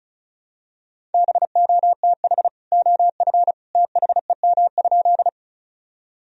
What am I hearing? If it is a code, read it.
Morse code, heard as a keyed tone, about 36 wpm, 700 Hz. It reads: BOTH OF THEM?
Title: Sentences from Top 1000 Words - Review Part 1 - 35wpm